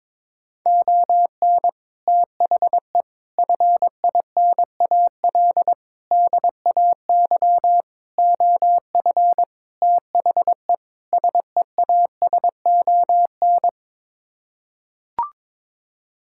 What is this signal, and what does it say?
Morse code, heard as a keyed tone, about 21 wpm, 700 Hz. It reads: ON THE FINAL DAY OF THE SEASON
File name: practice / 22wpm / Sets of 7 Words - 22wpm